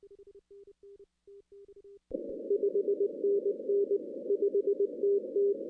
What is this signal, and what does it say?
Morse code, heard as a keyed tone, about 19 wpm, 380 Hz. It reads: 5NN 5TT
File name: 2016 arrl dx cw / 2016-ARRL-DX-CW-K5TR- - 0-26